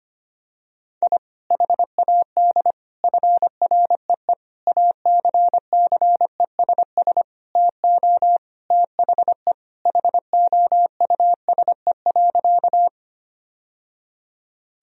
Morse code, heard as a keyed tone, about 25 wpm, 700 Hz. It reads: I HAD FREE ACCESS TO THE HOUSE.
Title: Sentences from Top 1000 Words - Review Part 4 - 25wpm